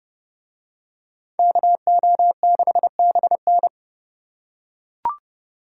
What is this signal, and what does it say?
Morse code, heard as a keyed tone, about 27 wpm, 700 Hz. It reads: KO6BN